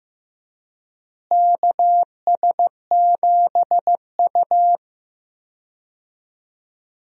Morse code, heard as a keyed tone, about 15 wpm, 700 Hz. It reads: KS7U
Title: USA Call Signs - 4 Characters - Repeated 3x - 15wpm